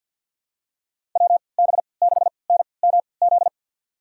Morse code, heard as a keyed tone, about 52 wpm, spaced as for 29 wpm, 700 Hz. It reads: WB6NMZ
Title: USA Call Signs - 6 Characters - 3x Character Spacing - 50wpm